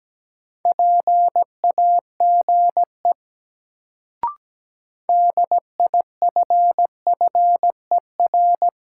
Morse code, heard as a keyed tone, about 17 wpm, 700 Hz. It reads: PAGE DIFFER
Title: Sentences from Top 1000 Words - Review Part 1 - New Word Drill - 17wpm